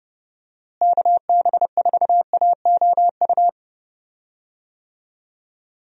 Morse code, heard as a keyed tone, about 30 wpm, 700 Hz. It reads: KB4AOU
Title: USA Call Signs - 6 Characters - Repeated 3x - 30wpm